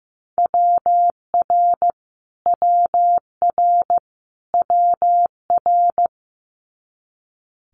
Morse code, heard as a keyed tone, about 15 wpm, 700 Hz. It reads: WR WR WR
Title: Multi-letter Phonemes Repeated 3x 15wpm